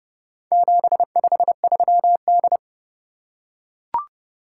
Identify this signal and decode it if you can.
Morse code, heard as a keyed tone, about 27 wpm, 700 Hz. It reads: 753D